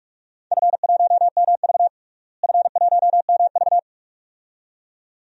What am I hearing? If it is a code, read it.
Morse code, heard as a keyed tone, about 41 wpm, 700 Hz. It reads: F1MV F1MV
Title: DXCC-100 Call Signs - 4 Characters - Repeated 2x - 45wpm